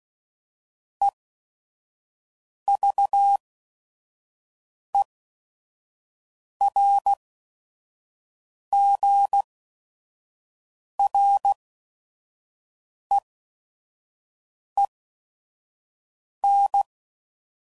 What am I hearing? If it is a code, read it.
Morse code, heard as a keyed tone, about 16 wpm, spaced as for 5 wpm, 800 Hz. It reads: EVERGREEN
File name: CW Code Groups @5WPM / fromSM's / Cities